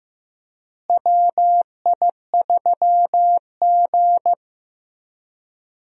Morse code, heard as a keyed tone, about 15 wpm, 700 Hz. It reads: WI3G
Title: USA Call Signs - 4 Characters - Repeated 3x - 15wpm